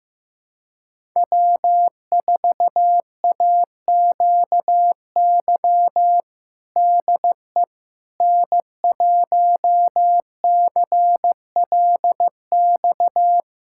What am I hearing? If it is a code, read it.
Morse code, heard as a keyed tone, about 16 wpm, 700 Hz. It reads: W4AQY DE N1CLX
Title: USA Callsign DE Callsign - 5 Characters - 15wpm